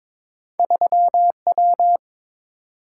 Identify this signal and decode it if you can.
Morse code, heard as a keyed tone, about 22 wpm, 700 Hz. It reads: 3W
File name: practice / 22wpm / Two Letter-Number Combinations - 22wpm